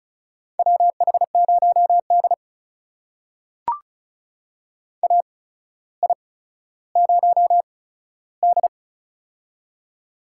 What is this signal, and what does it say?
Morse code, heard as a keyed tone, about 36 wpm, spaced as for 15 wpm, 700 Hz. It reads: WH0D AI0D
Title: USA Call Signs - 4 Characters - 8x Character Spacing - 35wpm